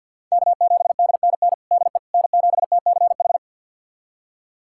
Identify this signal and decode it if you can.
Morse code, heard as a keyed tone, about 51 wpm, 700 Hz. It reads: K7DNN DE N7TCH
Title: USA Callsign DE Callsign - 5 Characters - 50wpm